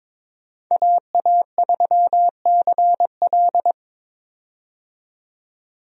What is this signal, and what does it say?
Morse code, heard as a keyed tone, about 22 wpm, 700 Hz. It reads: AA3CL
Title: USA Call Signs - 5 Characters - 22wpm